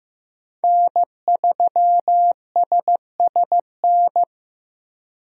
Morse code, heard as a keyed tone, about 15 wpm, 700 Hz. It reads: N3SSN
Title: USA Call Signs - 5 Characters - Repeated 3x - 15wpm